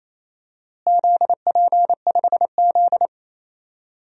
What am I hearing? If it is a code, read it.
Morse code, heard as a keyed tone, about 28 wpm, 700 Hz. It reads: ZP5Z
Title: DXCC-100 Call Signs - 4 Characters - Repeated 2x - 28wpm